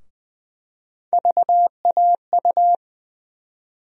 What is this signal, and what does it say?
Morse code, heard as a keyed tone, about 20 wpm, 700 Hz. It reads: VAU
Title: USA Call Sign Postfixes - 3 Letters 20wpm